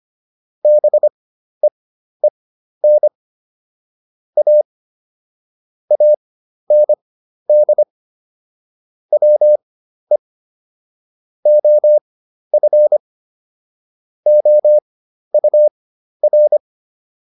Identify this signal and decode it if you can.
Morse code, heard as a keyed tone, about 25 wpm, spaced as for 12 wpm, 600 Hz. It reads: BEEN A AND WE OF OUR